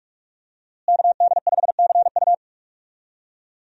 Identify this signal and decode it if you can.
Morse code, heard as a keyed tone, about 46 wpm, 700 Hz. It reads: KD5CU
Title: USA Call Signs - 5 Characters - Repeated 3x - 45wpm